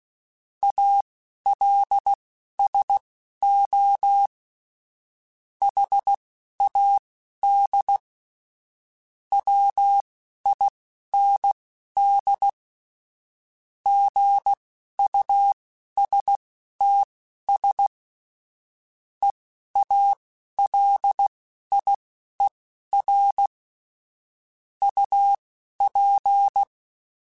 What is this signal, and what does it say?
Morse code, heard as a keyed tone, about 16 wpm, spaced as for 11 wpm, 800 Hz. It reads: ALSO HAD WIND GUSTS EALIER UP
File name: Weather Sets / Weather Sets-mp3s / Weather 5 5 @10